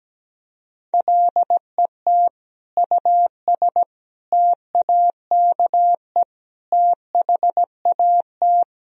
Morse code, heard as a keyed tone, about 17 wpm, 700 Hz. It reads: LET US TAKE THAT
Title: Sentences from Top 1000 Words - Review Part 2 - 17wpm